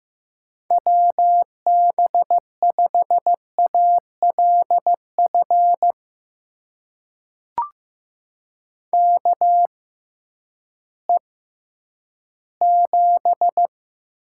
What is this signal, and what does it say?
Morse code, heard as a keyed tone, about 15 wpm, spaced as for 10 wpm, 700 Hz. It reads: WB5ALF KE7
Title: USA Call Signs - 6 Characters - 6x Character Spacing - 15wpm